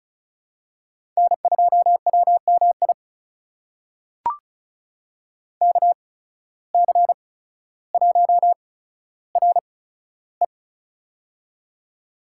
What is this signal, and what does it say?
Morse code, heard as a keyed tone, about 34 wpm, spaced as for 15 wpm, 700 Hz. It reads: N2WMI KC1RE
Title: USA Call Signs - 5 Characters - 8x Character Spacing - 35wpm